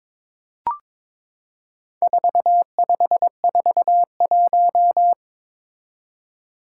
Morse code, heard as a keyed tone, about 22 wpm, 700 Hz. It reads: 4541
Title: Four Challenging Letter-Number Combinations 22wpm